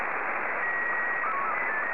SSTV (slow-scan television), Scottie 1.